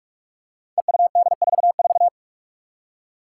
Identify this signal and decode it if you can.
Morse code, heard as a keyed tone, about 41 wpm, 700 Hz. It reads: EUD44
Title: Five Challenging Letter-Number Combinations 45wpm